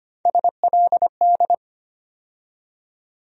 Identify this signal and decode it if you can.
Morse code, heard as a keyed tone, about 25 wpm, 700 Hz. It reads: SLD
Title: USA Call Sign Postfixes - 3 Letters 25wpm